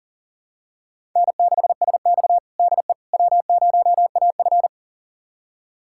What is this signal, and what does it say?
Morse code, heard as a keyed tone, about 40 wpm, 700 Hz. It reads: N6SX DE W0AF
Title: USA Callsign DE Callsign - 4 Characters - 40wpm